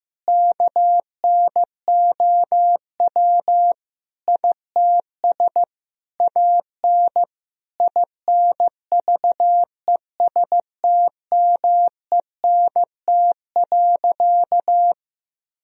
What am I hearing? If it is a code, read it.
Morse code, heard as a keyed tone, about 15 wpm, 700 Hz. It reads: KNOW ITS AN INVESTMENT.
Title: Sentences from Top 1000 Words - Review Part 4 - 15wpm